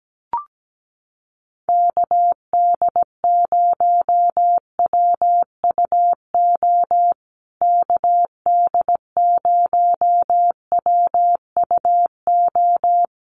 Morse code, heard as a keyed tone, about 17 wpm, 700 Hz. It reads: KD0WUO KD0WUO
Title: USA Call Signs Repeated 3x 17wpm